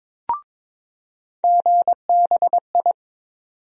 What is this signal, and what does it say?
Morse code, heard as a keyed tone, about 21 wpm, 700 Hz. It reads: GBI
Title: USA Call Sign Postfixes - 3 Letters 22wpm